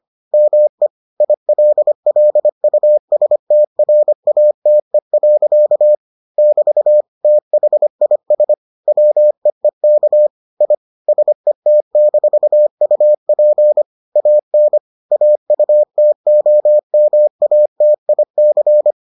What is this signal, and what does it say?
Morse code, heard as a keyed tone, about 25 wpm, 600 Hz. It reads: ME ILLUSTRATE. <BT> THIS WEEK I SET-UP AN AUTOMATIC